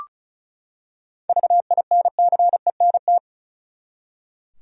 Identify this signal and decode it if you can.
Morse code, heard as a keyed tone, about 36 wpm, 700 Hz. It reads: VINCENT
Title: USA Common Names - 35wpm